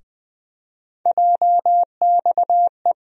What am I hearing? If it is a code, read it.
Morse code, heard as a keyed tone, about 21 wpm, 700 Hz. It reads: JXE
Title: USA Call Sign Postfixes - 3 Letters 20wpm